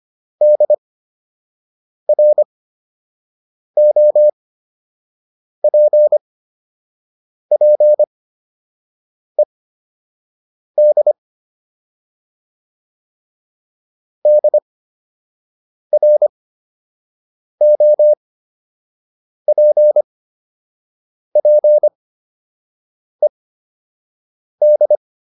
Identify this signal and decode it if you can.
Morse code, heard as a keyed tone, about 25 wpm, spaced as for 6 wpm, 600 Hz. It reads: DROPPED DROPPED